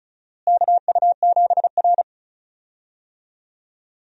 Morse code, heard as a keyed tone, about 36 wpm, 700 Hz. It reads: KU7R
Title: USA Call Signs - 4 Characters - Repeated 2x - 35wpm